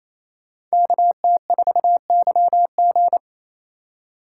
Morse code, heard as a keyed tone, about 35 wpm, 700 Hz. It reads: KT4YG